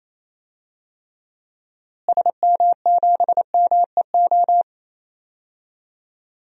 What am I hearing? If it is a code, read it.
Morse code, heard as a keyed tone, about 27 wpm, 700 Hz. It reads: SM7MEO
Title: DXCC-100 Call Signs - 6 Characters - Repeated 3x - 28wpm